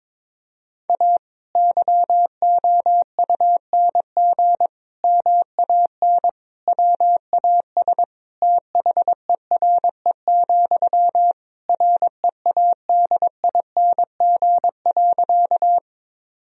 Morse code, heard as a keyed tone, about 22 wpm, 700 Hz. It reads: A YOUNG MAN WAS THERE, READING.
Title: Sentences from Top 1000 Words - Review Part 3 - 22wpm